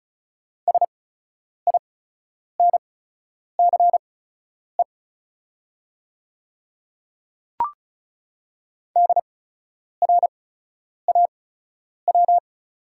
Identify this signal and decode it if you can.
Morse code, heard as a keyed tone, about 36 wpm, spaced as for 10 wpm, 700 Hz. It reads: SINCE DRAW
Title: 8x Character Spacing - Top 500 Words - 35wpm